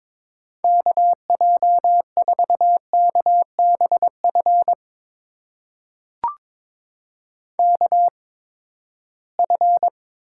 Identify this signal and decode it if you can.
Morse code, heard as a keyed tone, about 22 wpm, spaced as for 15 wpm, 700 Hz. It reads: KJ4KBF KF
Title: USA Call Signs - 6 Characters - 8x Character Spacing - 22wpm